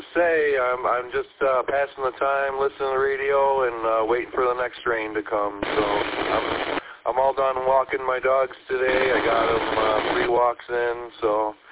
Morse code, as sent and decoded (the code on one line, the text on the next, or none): none